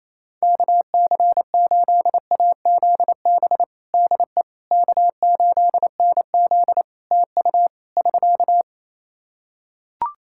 Morse code, heard as a keyed tone, about 28 wpm, 700 Hz. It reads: KC8AZB DE K8NZ TU <SK>
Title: Practice QSO - Exchange 4 - Ending - 28wpm